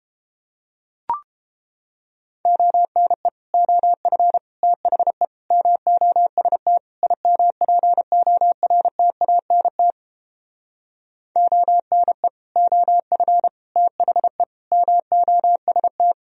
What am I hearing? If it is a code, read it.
Morse code, heard as a keyed tone, about 31 wpm, 700 Hz. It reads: ONE OF THE MOST IMPORTANT ONE OF THE MOST